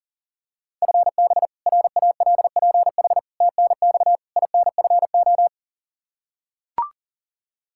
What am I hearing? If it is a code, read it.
Morse code, heard as a keyed tone, about 37 wpm, 700 Hz. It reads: FB RALPH TNX INFO
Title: Practice QSO - Exchange 3 - More Info - 40wpm